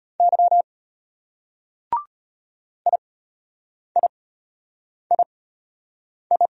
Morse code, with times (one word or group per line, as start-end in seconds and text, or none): 0.2-0.6 Y
2.9-6.5 IIII